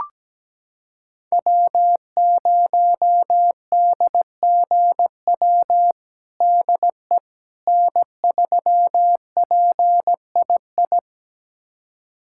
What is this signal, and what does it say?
Morse code, heard as a keyed tone, about 17 wpm, 700 Hz. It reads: W0DGW DE N3PII
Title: USA Callsign DE Callsign - 5 Characters - 17wpm